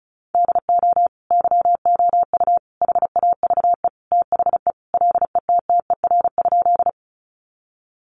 Morse code, heard as a keyed tone, about 34 wpm, 700 Hz. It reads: DO YOU HAVE THE LETTER?
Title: Sentences from Top 300 Words 35wpm